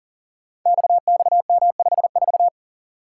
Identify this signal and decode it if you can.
Morse code, heard as a keyed tone, about 43 wpm, 700 Hz. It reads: XXM54
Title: Code Groups - 5 Letter:Numbers - 40wpm